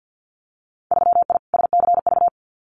Morse code, heard as a keyed tone, about 50 wpm, 700 Hz. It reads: 3I SCV
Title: Winter Field Day Exchanges Encore 50wpm